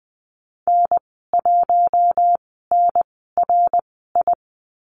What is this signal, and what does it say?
Morse code, heard as a keyed tone, about 20 wpm, spaced as for 15 wpm, 700 Hz. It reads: N1NRI
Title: 055 Lesson 10g - Letter r - Call Signs 20wpm